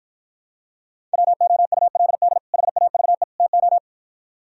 Morse code, heard as a keyed tone, about 54 wpm, 700 Hz. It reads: WOULD HAVE TO